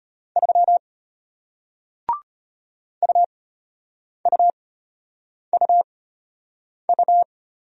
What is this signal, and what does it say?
Morse code, heard as a keyed tone, about 34 wpm, spaced as for 8 wpm, 700 Hz. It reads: 3 UUUU